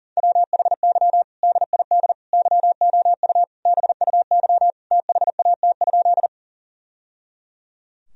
Morse code, heard as a keyed tone, about 40 wpm, 700 Hz. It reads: WHY DID YOU BUY THAT?
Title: Sentences from Top 1000 Words - Review Part 4 - 40wpm